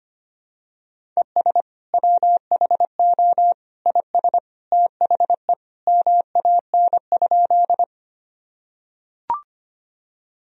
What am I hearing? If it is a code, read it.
Morse code, heard as a keyed tone, about 25 wpm, 700 Hz. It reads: ES WHO IS THE MAN?
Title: Sentences from Top 200 Words and Abbreviations 25wpm